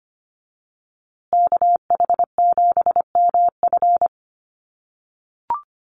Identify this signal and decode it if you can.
Morse code, heard as a keyed tone, about 24 wpm, 700 Hz. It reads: KH7MF
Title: USA Call Signs Repeated 2x 25wpm